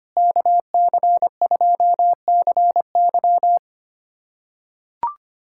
Morse code, heard as a keyed tone, about 25 wpm, 700 Hz. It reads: KC2CY
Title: USA Callsign DE Callsign - 5 Characters - 25wpm